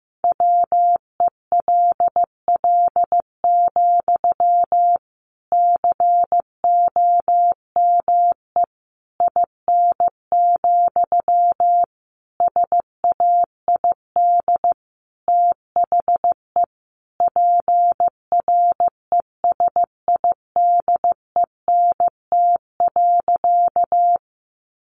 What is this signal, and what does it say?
Morse code, heard as a keyed tone, about 15 wpm, 700 Hz. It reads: WELL, COME IN, SAID THE PRESIDENT.
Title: Sentences from Top 1000 Words 15wpm